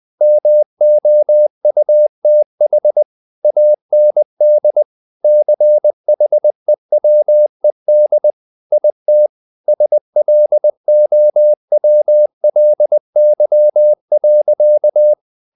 Morse code, heard as a keyed tone, about 20 wpm, 600 Hz. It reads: MOUTH AND CHEWED IT SLOWLY.
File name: the old man and the sea / the old man and the sea part 1 20wpm